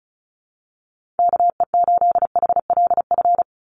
Morse code, heard as a keyed tone, about 34 wpm, 700 Hz. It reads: KE8HLF
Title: USA Call Signs 35wpm